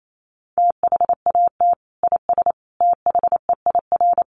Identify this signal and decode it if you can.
Morse code, heard as a keyed tone, about 28 wpm, 700 Hz. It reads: THAT IS THEIR
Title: Sentences from Top 1300 Words 28wpm